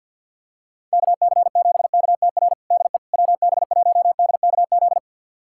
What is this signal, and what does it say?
Morse code, heard as a keyed tone, about 52 wpm, 700 Hz. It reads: KC7KTR DE WB1DKZ
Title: USA Callsign DE Callsign 50wpm